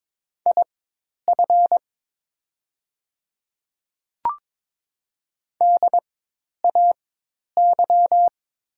Morse code, heard as a keyed tone, about 22 wpm, spaced as for 10 wpm, 700 Hz. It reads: IF DAY
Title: 4x Character Spacing - Top 100 Words - 22wpm